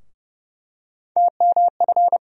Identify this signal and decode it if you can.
Morse code, heard as a keyed tone, about 30 wpm, 700 Hz. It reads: TMF
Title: USA Call Sign Postfixes - 3 Letters 30wpm